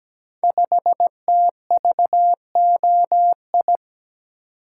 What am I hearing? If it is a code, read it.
Morse code, heard as a keyed tone, about 17 wpm, 700 Hz. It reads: 5TVOI
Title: Code Groups - 5 Letter:Numbers - 17wpm